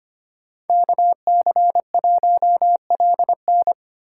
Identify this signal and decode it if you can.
Morse code, heard as a keyed tone, about 27 wpm, 700 Hz. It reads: KC1LN
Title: Speed Racing - USA 2x2 Call Signs 25wpm